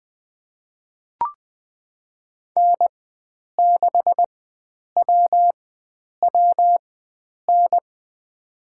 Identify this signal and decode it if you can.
Morse code, heard as a keyed tone, about 20 wpm, spaced as for 9 wpm, 700 Hz. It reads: N6WWN
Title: USA Call Signs - 5 Characters - 4x Character Spacing - 20wpm